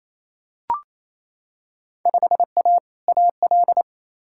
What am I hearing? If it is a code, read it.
Morse code, heard as a keyed tone, about 28 wpm, 700 Hz. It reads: 5A AL
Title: Field Day Exchanges Encore - 28wpm